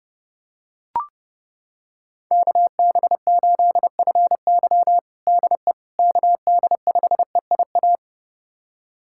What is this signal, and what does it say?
Morse code, heard as a keyed tone, about 30 wpm, 700 Hz. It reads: KB8FY DE KD5EIA
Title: USA Callsign DE Callsign 30wpm